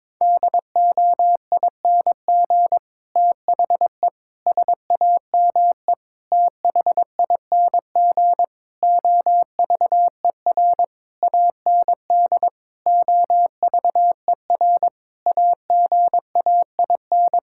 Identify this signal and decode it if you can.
Morse code, heard as a keyed tone, about 22 wpm, 700 Hz. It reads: DOING THE SAME THING OVER AND OVER AGAIN
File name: Sets of 9 Words - 22wpm